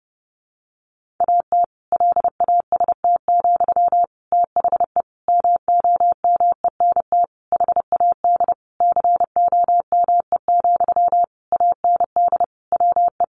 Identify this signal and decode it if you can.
Morse code, heard as a keyed tone, about 30 wpm, 700 Hz. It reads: AT LAST, THE MOMENT HAD COME, AND WE
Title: Sentences from Top 1500 Words 30wpm